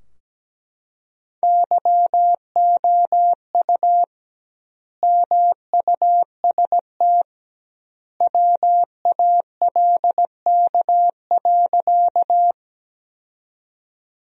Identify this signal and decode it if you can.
Morse code, heard as a keyed tone, about 17 wpm, 700 Hz. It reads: YOU MUST WALK.
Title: 2x Word-Spacing - Sentences from Top 300 Words - 17wpm